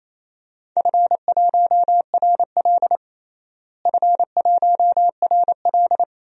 Morse code, heard as a keyed tone, about 27 wpm, 700 Hz. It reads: F1RL F1RL
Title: DXCC-100 Call Signs - 4 Characters - Repeated 2x - 28wpm